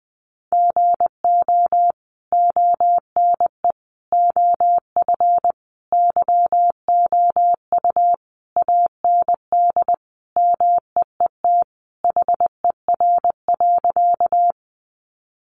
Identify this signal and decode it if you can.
Morse code, heard as a keyed tone, about 20 wpm, 700 Hz. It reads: GO ONE OF YOU AND MEET HER.
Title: Sentences from Top 1100 Words 20wpm